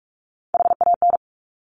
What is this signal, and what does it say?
Morse code, heard as a keyed tone, about 46 wpm, 700 Hz. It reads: HAN
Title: Top Three Letter Combinations in Words 45wpm